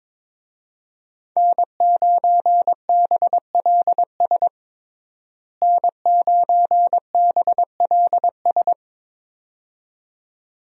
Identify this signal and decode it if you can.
Morse code, heard as a keyed tone, about 22 wpm, 700 Hz. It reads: N9BLS N9BLS
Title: USA Call Signs - 5 Characters - Repeated 2x - 22wpm